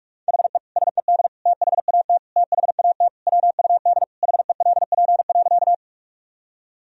Morse code, heard as a keyed tone, about 41 wpm, 700 Hz. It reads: HE SED THAT THAT WUD HELP.
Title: Sentences from Top 200 Words and Abbreviations 45wpm